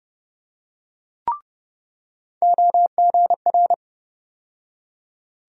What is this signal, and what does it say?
Morse code, heard as a keyed tone, about 30 wpm, 700 Hz. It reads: OGR
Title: USA Call Sign Postfixes - 3 Letters 30wpm